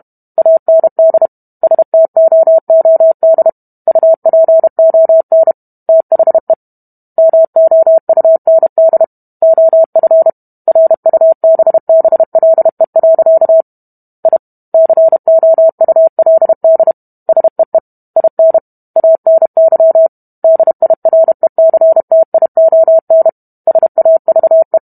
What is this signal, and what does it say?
Morse code, heard as a keyed tone, about 32 wpm, 650 Hz. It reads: AND STOOD UPON THE MOUND OF RUBBLE. I COULD SEE IN ANY DIRECTION SAVE